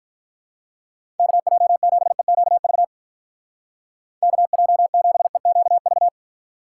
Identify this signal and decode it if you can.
Morse code, heard as a keyed tone, about 43 wpm, 700 Hz. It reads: KJ7EQV KJ7EQV
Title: Speed Racing - USA 2x3 Call Signs 35wpm